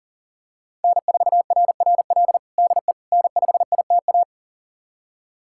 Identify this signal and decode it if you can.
Morse code, heard as a keyed tone, about 40 wpm, 700 Hz. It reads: N4RRL DE N5ITA
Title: USA Callsign DE Callsign - 5 Characters - 40wpm